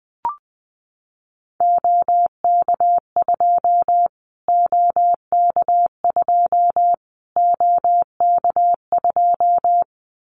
Morse code, heard as a keyed tone, about 20 wpm, 700 Hz. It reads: OK2 OK2 OK2
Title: Callsign Prefixes for Top 25 DXCC - Repeated 3x - 20wpm